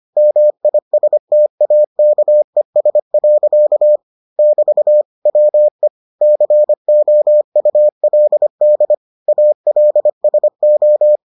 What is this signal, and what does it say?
Morse code, heard as a keyed tone, about 25 wpm, 600 Hz. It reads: MISTAKES. <BT> WE COULD ALSO